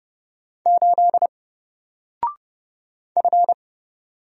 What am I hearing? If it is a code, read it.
Morse code, heard as a keyed tone, about 30 wpm, 700 Hz. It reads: 8 F
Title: Single Character (Letter:Number:Punctuation:Prosign) 30wpm